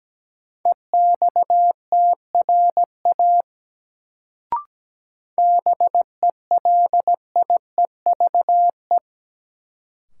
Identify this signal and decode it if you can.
Morse code, heard as a keyed tone, about 17 wpm, 700 Hz. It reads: EXTRA BELIEVE
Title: Sentences from Top 1000 Words - New Word Drill - 17wpm